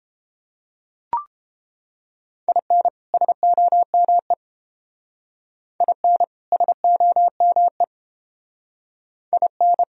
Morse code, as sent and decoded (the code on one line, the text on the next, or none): .. -. / ... --- -- . / .. -. / ... --- -- . / .. -.
IN SOME IN SOME IN